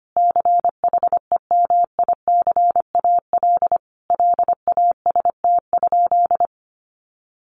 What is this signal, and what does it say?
Morse code, heard as a keyed tone, about 25 wpm, 700 Hz. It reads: CHEMICAL LAST?
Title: Sentences from Top 1200 Words 25wpm